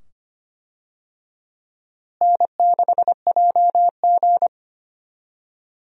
Morse code, heard as a keyed tone, about 25 wpm, 700 Hz. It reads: N6JG
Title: USA Call Signs - 4 Characters - Repeated 2x - 25wpm